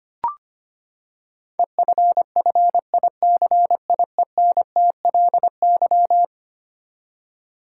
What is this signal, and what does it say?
Morse code, heard as a keyed tone, about 25 wpm, 700 Hz. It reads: EFFICIENTLY
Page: morse-code-ninja-practice-25wpm